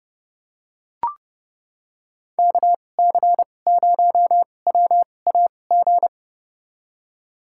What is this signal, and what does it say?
Morse code, heard as a keyed tone, about 32 wpm, spaced as for 22 wpm, 700 Hz. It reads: KC0WAG
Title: USA Call Signs - 6 Characters - 2x Character Spacing - 30wpm